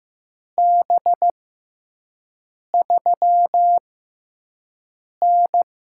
Morse code, heard as a keyed tone, about 15 wpm, spaced as for 5 wpm, 700 Hz. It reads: B3N
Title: USA Call Signs - 4 Characters - 6x Character Spacing - 15wpm